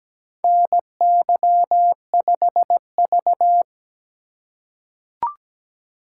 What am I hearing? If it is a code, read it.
Morse code, heard as a keyed tone, about 17 wpm, 700 Hz. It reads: NY5V
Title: USA Call Signs - 4 Characters - 3x Character Spacing - 17wpm